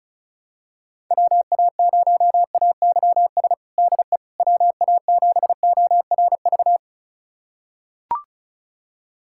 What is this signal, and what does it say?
Morse code, heard as a keyed tone, about 34 wpm, 700 Hz. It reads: WA0AYS DE WA7ORV
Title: USA Callsign DE Callsign - 6 Characters - 35wpm